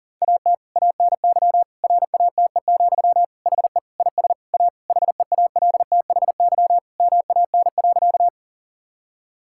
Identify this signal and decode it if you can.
Morse code, heard as a keyed tone, about 37 wpm, 700 Hz. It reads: AT ANY RATE, HE IS A HEALTHY MAN.
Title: Sentences from Top 1000 Words - Review Part 5 - 40wpm